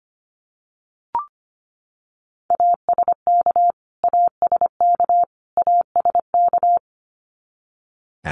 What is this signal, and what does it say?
Morse code, heard as a keyed tone, about 25 wpm, 700 Hz. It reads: ASK ASK ASK
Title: Top 500 Words Repeated 25wpm